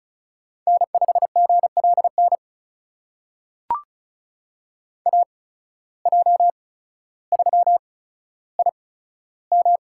Morse code, heard as a keyed tone, about 36 wpm, spaced as for 15 wpm, 700 Hz. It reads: N5GLN AJ3IM
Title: USA Call Signs - 5 Characters - 8x Character Spacing - 35wpm